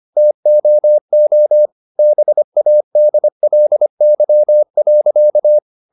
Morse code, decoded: TOO BADLY.

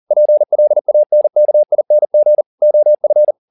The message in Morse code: PLANKING OF

